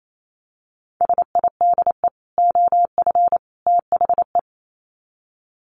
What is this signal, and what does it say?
Morse code, heard as a keyed tone, about 28 wpm, 700 Hz. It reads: SIDE OF THE
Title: Sets of 3 Words 28wpm